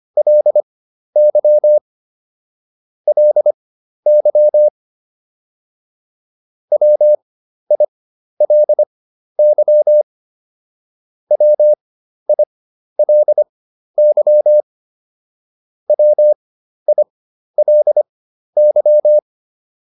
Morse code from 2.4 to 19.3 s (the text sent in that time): LY WILY WILY WILY